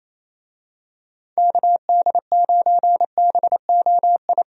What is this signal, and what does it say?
Morse code, heard as a keyed tone, about 28 wpm, 700 Hz. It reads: KD9BOI
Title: USA Call Signs - 6 Characters - 28wpm